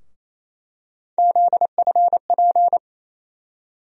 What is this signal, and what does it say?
Morse code, heard as a keyed tone, about 28 wpm, 700 Hz. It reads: ZFP
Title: USA Call Sign Postfixes - 3 Letters 28wpm